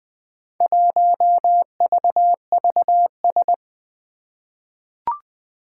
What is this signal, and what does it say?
Morse code, heard as a keyed tone, about 20 wpm, 700 Hz. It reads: 1VVS
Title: Four Challenging Letter-Number Combinations 20wpm